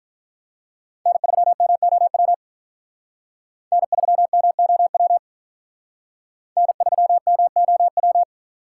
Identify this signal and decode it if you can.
Morse code, heard as a keyed tone, about 46 wpm, 700 Hz. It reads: N3MOW N3MOW N3MOW